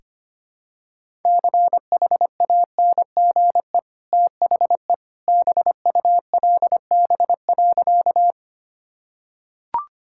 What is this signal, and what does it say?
Morse code, heard as a keyed tone, about 25 wpm, 690 Hz. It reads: CHANGE THE BULB.
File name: Easy Sentences with Hints - 25wpm